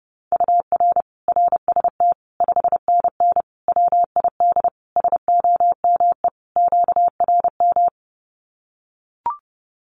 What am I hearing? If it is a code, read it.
Morse code, heard as a keyed tone, about 30 wpm, 700 Hz. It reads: UR RST 5NN WID SOME QRM
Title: Common QSO Phrases 30wpm